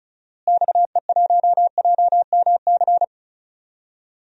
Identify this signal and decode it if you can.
Morse code, heard as a keyed tone, about 34 wpm, 700 Hz. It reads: XE1JMC
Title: DXCC-100 Call Signs - 6 Characters - Repeated 2x - 35wpm